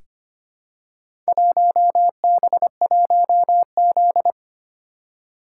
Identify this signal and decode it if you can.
Morse code, heard as a keyed tone, about 25 wpm, 700 Hz. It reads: 1B1Z
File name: practice / 25wpm / Four Letter-Number Combinations - 25wpm